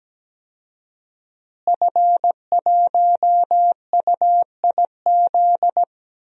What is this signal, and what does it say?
Morse code, heard as a keyed tone, about 17 wpm, 700 Hz. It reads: F1UIZ
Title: DXCC-100 Call Signs - 5 Characters - Repeated 2x - 17wpm